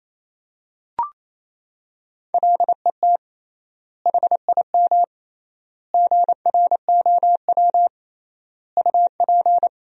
Morse code, heard as a keyed tone, about 28 wpm, 700 Hz. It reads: LET HIM GROW UP
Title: 3x Word-Spacing - Sentences from Top 300 Words - 28wpm